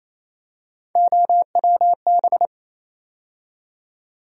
Morse code, heard as a keyed tone, about 28 wpm, 700 Hz. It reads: OWB
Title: USA Call Sign Postfixes - 3 Letters 28wpm